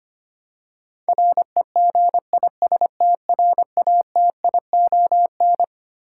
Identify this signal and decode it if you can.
Morse code, heard as a keyed tone, about 25 wpm, 700 Hz. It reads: REGISTRATION
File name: Twelve Letter Words - 25wpm